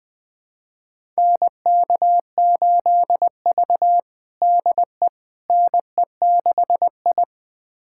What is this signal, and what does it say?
Morse code, heard as a keyed tone, about 21 wpm, 700 Hz. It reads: NK8V DE NE6I
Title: USA Callsign DE Callsign - 4 Characters - 20wpm